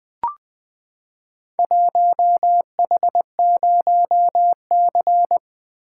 Morse code, heard as a keyed tone, about 20 wpm, 700 Hz. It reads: 1H0C
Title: Four Letter-Number Combinations - 20wpm